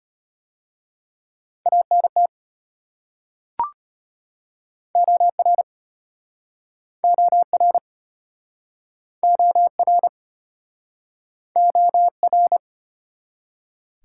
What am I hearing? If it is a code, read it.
Morse code, heard as a keyed tone, about 34 wpm, 700 Hz. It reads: ANT OR OR OR OR